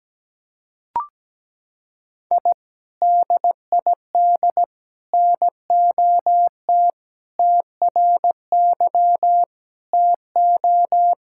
Morse code, heard as a keyed tone, about 17 wpm, 700 Hz. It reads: I DID NOT TRY TO
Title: Sentences from Top 1000 Words - Review Part 5 - 17wpm